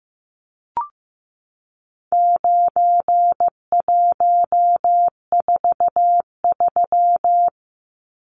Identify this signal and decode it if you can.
Morse code, heard as a keyed tone, about 15 wpm, 700 Hz. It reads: 9143
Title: Numbers - 4-Digit 15wpm